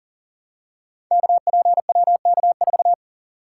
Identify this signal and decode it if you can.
Morse code, heard as a keyed tone, about 40 wpm, 700 Hz. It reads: KPWK4
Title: Code Groups - 5 Letter:Numbers - 40wpm